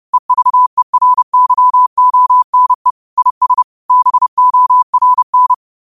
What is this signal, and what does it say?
Morse code, heard as a keyed tone, about 30 wpm, 1000 Hz. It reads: EVERYONE IS BORN
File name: qotdfiles / 30wpm